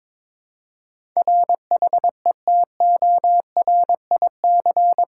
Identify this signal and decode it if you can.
Morse code, heard as a keyed tone, about 22 wpm, 700 Hz. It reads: RHETORIC
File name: practice / 22wpm / Eight Letter Words - 22wpm